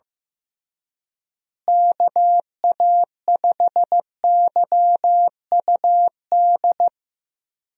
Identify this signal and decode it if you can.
Morse code, heard as a keyed tone, about 15 wpm, 700 Hz. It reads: KA5YUD
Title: USA Call Signs - 6 Characters - Repeated 2x - 15wpm